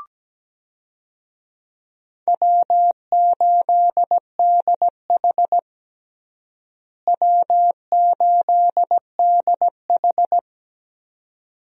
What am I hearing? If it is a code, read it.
Morse code, heard as a keyed tone, about 17 wpm, 700 Hz. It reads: W8DH W8DH
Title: USA Call Signs - 4 Characters - Repeated 2x - 17wpm